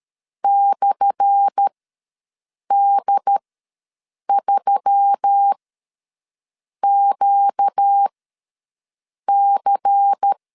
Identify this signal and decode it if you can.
Morse code, heard as a keyed tone, about 13 wpm, spaced as for 6 wpm, 800 Hz. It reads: /D3QC
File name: LEZIONI / 10 SIGNS / 10B INT ws 12wpm 3527